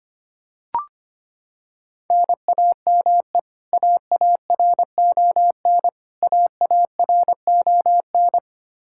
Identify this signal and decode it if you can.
Morse code, heard as a keyed tone, about 25 wpm, 700 Hz. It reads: NAME AARON AARON
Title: Practice QSO - Exchange 1 - Beginning - 25wpm